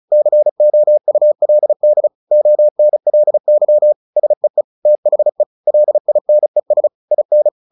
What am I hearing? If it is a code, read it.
Morse code, heard as a keyed tone, about 34 wpm, 600 Hz. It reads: COULD ONLY SEE THE LINES IN
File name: the old man and the sea 2 35wpm